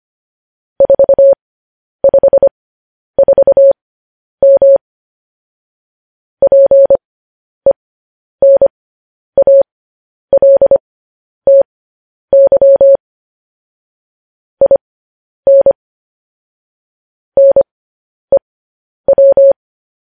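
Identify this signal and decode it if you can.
Morse code, heard as a keyed tone, about 25 wpm, spaced as for 10 wpm, 550 Hz. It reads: 454M PENALTY IN NEW
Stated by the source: Morse code news headlines broadcast.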